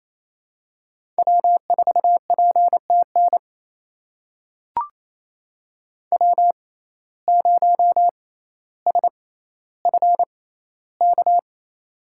Morse code, heard as a keyed tone, about 28 wpm, spaced as for 14 wpm, 700 Hz. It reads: W4PTN W0SFK
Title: USA Call Signs - 5 Characters - 6x Character Spacing - 28wpm